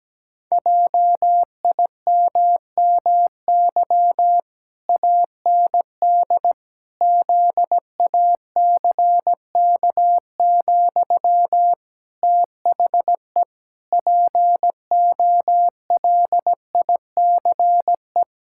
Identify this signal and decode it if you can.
Morse code, heard as a keyed tone, about 17 wpm, 700 Hz. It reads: JIMMY AND ZACK, THE POLICE
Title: Pangram Challenge v2 - 17wpm